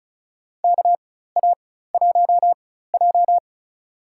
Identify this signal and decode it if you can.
Morse code, heard as a keyed tone, about 34 wpm, spaced as for 16 wpm, 700 Hz. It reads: KA1J
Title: USA Call Signs - 4x Character Spacing - 35wpm